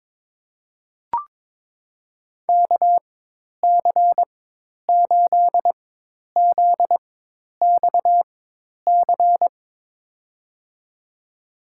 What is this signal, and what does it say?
Morse code, heard as a keyed tone, about 22 wpm, spaced as for 10 wpm, 700 Hz. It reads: KC8ZXC
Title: USA Call Signs - 6 Characters - 4x Character Spacing - 22wpm